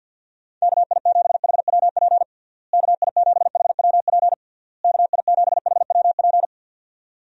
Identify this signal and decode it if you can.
Morse code, heard as a keyed tone, about 50 wpm, 700 Hz. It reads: KI7HWP KI7HWP KI7HWP
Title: USA Call Signs - 6 Characters - Repeated 3x - 50wpm